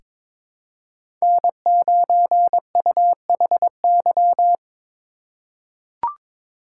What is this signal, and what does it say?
Morse code, heard as a keyed tone, about 22 wpm, 700 Hz. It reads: N9UHY
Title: USA Call Signs - 5 Characters - 3x Character Spacing - 22wpm